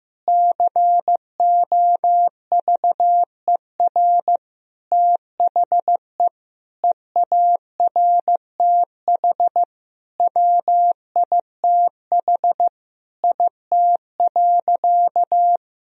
Morse code, heard as a keyed tone, about 15 wpm, 700 Hz. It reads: COVER THE EARTH WITH IT.
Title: Sentences from Top 1000 Words - Review Part 1 - 15wpm